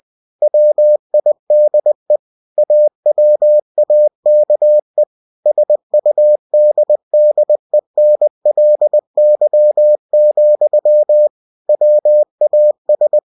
Morse code, heard as a keyed tone, about 20 wpm, 600 Hz. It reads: WIDE AWAKE SUDDENLY, WAS